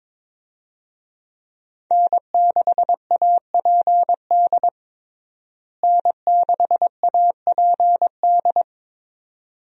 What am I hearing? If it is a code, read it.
Morse code, heard as a keyed tone, about 22 wpm, 700 Hz. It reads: N6APD N6APD
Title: USA Call Signs - 5 Characters - Repeated 2x - 22wpm